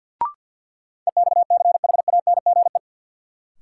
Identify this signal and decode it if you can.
Morse code, heard as a keyed tone, about 49 wpm, 700 Hz. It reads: EXCHANGE
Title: Sentences from Top 1000 Words - Review Part 6 - New Word Drill - 50wpm